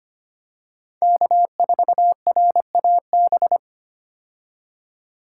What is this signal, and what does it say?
Morse code, heard as a keyed tone, about 25 wpm, 700 Hz. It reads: K4RAB
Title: K1USN SST QSO Practice - 25wpm